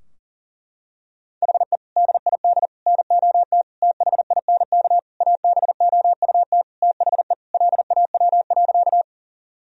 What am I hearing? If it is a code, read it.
Morse code, heard as a keyed tone, about 40 wpm, 700 Hz. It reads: HE DID NOT THINK ABOUT THE LAW.
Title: Sentences from Top 1000 Words - Review Part 3 - 40wpm